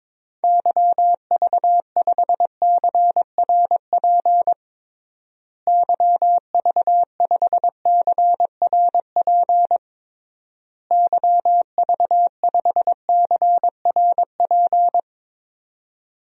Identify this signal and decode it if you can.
Morse code, heard as a keyed tone, about 22 wpm, 700 Hz. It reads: YV5CRP YV5CRP YV5CRP
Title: DXCC-100 Call Signs - 6 Characters - Repeated 3x - 22wpm